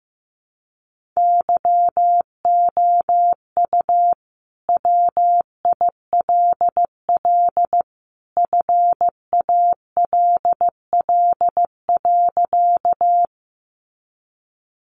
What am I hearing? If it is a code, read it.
Morse code, heard as a keyed tone, about 15 wpm, 700 Hz. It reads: YOU WILL FALL.
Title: Sentences from Top 500 Words 15wpm